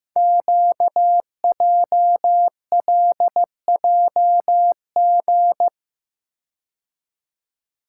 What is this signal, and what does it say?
Morse code, heard as a keyed tone, about 15 wpm, 700 Hz. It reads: QJLJG
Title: Code Groups - 5 Letters - 15wpm